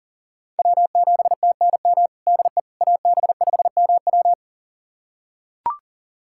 Morse code, heard as a keyed tone, about 40 wpm, 700 Hz. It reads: W7TNM DE AB5MW
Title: USA Callsign DE Callsign - 5 Characters - 40wpm